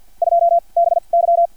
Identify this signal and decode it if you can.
Morse code, heard as a keyed tone, about 50 wpm, spaced as for 36 wpm, 680 Hz. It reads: 2 BY
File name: archived / CRSnet